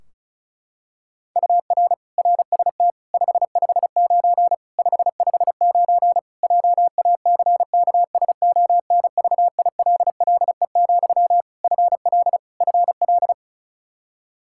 Morse code, heard as a keyed tone, about 36 wpm, 700 Hz. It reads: UR RST 559 559 JACKSONVILLE, FL FL
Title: Practice QSO - Exchange 1 - Beginning - 35wpm